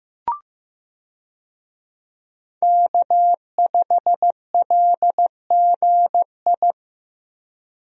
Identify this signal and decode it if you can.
Morse code, heard as a keyed tone, about 15 wpm, 700 Hz. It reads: K5LGI
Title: USA Call Signs - 5 Characters - Repeated 3x - 15wpm